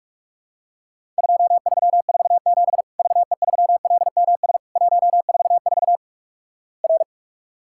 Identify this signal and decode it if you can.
Morse code, heard as a keyed tone, about 48 wpm, 700 Hz. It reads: 2347 VE3LMS 144 R
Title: Worked All Europe DX Contest - QTC Exchange - 45wpm